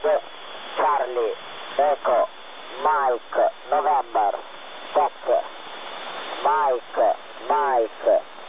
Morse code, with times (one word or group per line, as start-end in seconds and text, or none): none